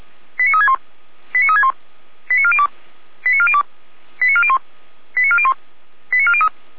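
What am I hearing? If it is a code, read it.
Five-tone selective call (ZVEI), heard as quick runs of five tones, tones of about 70 ms. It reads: 89361 89361 89402 89402 89401 89401 89403
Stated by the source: ZVEI five-tone selective call 89356 received by SDR.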